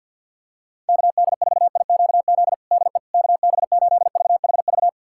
Morse code, heard as a keyed tone, about 46 wpm, 700 Hz. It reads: KD4IQZ DE KB8VHV
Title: USA Callsign DE Callsign - 6 Characters - 50wpm